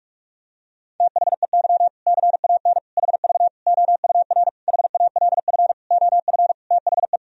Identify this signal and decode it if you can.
Morse code, heard as a keyed tone, about 46 wpm, 700 Hz. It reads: THEY CAN HV OUR HALF OF THE
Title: Sentences from Top 400 Words and Abbreviations 45wpm